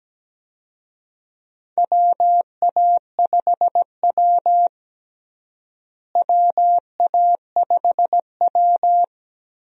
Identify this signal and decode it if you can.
Morse code, heard as a keyed tone, about 17 wpm, 700 Hz. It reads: WA5W WA5W
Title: USA Call Signs - 4 Characters - Repeated 2x - 17wpm